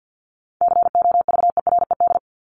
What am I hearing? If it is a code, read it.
Morse code, heard as a keyed tone, about 50 wpm, 700 Hz. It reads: COVERED
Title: ED Suffix - Top 100 Words with Base Words 50wpm